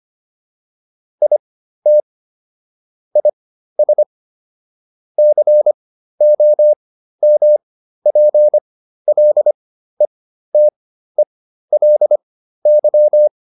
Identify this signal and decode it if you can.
Morse code, heard as a keyed tone, about 25 wpm, spaced as for 13 wpm, 600 Hz. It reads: IT IS COMPLETELY